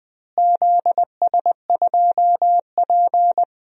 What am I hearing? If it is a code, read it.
Morse code, heard as a keyed tone, about 20 wpm, 700 Hz. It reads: ZS2P
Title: DXCC-100 Call Signs - 4 Characters - Repeated 2x - 20wpm